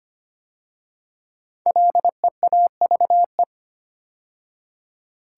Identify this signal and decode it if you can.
Morse code, heard as a keyed tone, about 25 wpm, 700 Hz. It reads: LEAVE
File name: 5 Letter Words - First 3 Letters Given - 25wpm